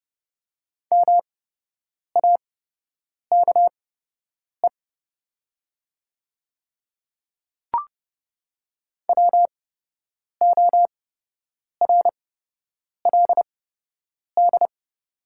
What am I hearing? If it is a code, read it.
Morse code, heard as a keyed tone, about 30 wpm, spaced as for 8 wpm, 690 Hz. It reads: MAKE WORLD
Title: 8x Character Spacing - Top 200 Words - 30wpm